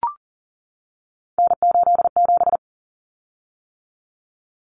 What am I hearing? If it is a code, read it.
Morse code, heard as a keyed tone, about 40 wpm, 700 Hz. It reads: N87
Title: Common Cut Numbers - 3-Digit 40wpm